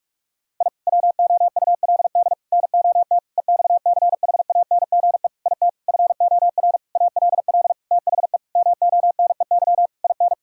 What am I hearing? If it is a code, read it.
Morse code, heard as a keyed tone, about 41 wpm, 700 Hz. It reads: I WOULD NOT EXCHANGE IT FOR ALL THE MONEY IN
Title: Sentences from Top 1000 Words - Review Part 6 - 45wpm